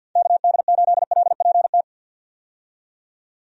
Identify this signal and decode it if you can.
Morse code, heard as a keyed tone, about 50 wpm, 700 Hz. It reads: KD8LPT
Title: USA Call Signs - 6 Characters - 50wpm